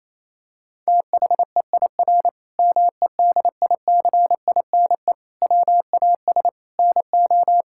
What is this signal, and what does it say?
Morse code, heard as a keyed tone, about 27 wpm, 700 Hz. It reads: THEIR MEDICINE WAS NO
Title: Sentences from Top 1000 Words - Review Part 6 - 28wpm